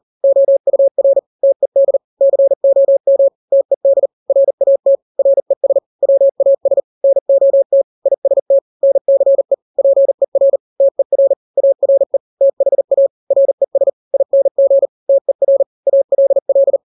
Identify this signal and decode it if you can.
Morse code, heard as a keyed tone, about 40 wpm, 550 Hz. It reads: OUR TED COM TED RAT RES WAS NOT IST NCE PER TER ARE THA RES ING TER ALL
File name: n-grams / 3-grams 40 35 WPM